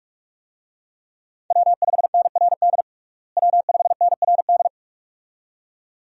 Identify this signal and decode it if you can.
Morse code, heard as a keyed tone, about 42 wpm, 700 Hz. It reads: W5NRD W5NRD
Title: USA Call Signs - 5 Characters - Repeated 2x - 45wpm